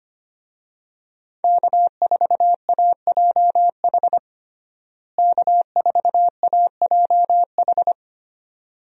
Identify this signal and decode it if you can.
Morse code, heard as a keyed tone, about 25 wpm, 700 Hz. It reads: K4AJH K4AJH
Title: USA Call Signs - 5 Characters - Repeated 2x - 25wpm